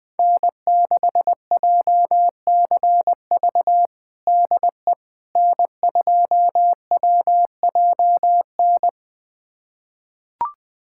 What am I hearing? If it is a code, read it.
Morse code, heard as a keyed tone, about 20 wpm, 700 Hz. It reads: N6JCV DE N2WJN
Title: USA Callsign DE Callsign - 5 Characters - 20wpm